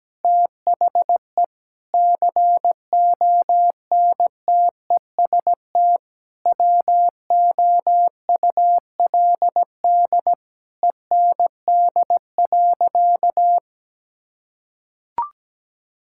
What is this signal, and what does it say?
Morse code, heard as a keyed tone, about 17 wpm, 700 Hz. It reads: THE CONTEST WOULD END.
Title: Sentences from Top 1600 Words 17wpm